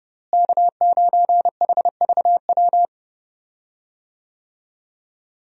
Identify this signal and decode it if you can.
Morse code, heard as a keyed tone, about 30 wpm, 700 Hz. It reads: K9HVW
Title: USA Call Signs - 5 Characters - Repeated 2x - 30wpm